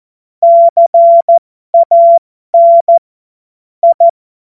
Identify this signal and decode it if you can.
Morse code, heard as a keyed tone, about 14 wpm, spaced as for 12 wpm, 680 Hz. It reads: CAN I
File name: Morsekurs / Witze